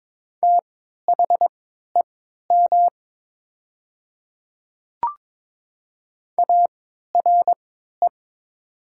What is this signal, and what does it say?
Morse code, heard as a keyed tone, about 22 wpm, spaced as for 12 wpm, 700 Hz. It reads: THEM ARE
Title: 3x Character Spacing - Top 100 Words - 22wpm